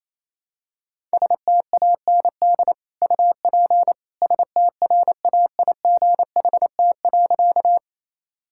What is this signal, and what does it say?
Morse code, heard as a keyed tone, about 28 wpm, 700 Hz. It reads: STAND UP STRAIGHT.
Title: Sentences from Top 1000 Words - Review Part 6 - 28wpm